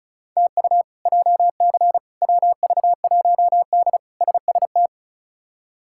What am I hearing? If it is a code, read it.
Morse code, heard as a keyed tone, about 36 wpm, 700 Hz. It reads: TU JC WV1D SST
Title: K1USN SST QSO Practice - 35wpm